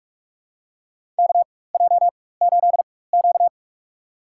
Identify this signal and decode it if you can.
Morse code, heard as a keyed tone, about 44 wpm, spaced as for 21 wpm, 690 Hz. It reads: KJ8Q
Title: USA Call Signs - 4 Characters - 4x Character Spacing - 45wpm